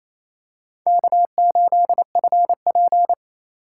Morse code, heard as a keyed tone, about 28 wpm, 700 Hz. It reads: K8FP
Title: USA Call Signs - 2x Character Spacing - 28wpm